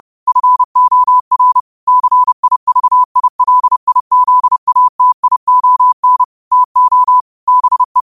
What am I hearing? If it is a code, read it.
Morse code, heard as a keyed tone, about 30 wpm, 1000 Hz. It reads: FOR CIVILIZATION TO BE
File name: qotdfiles / 30wpm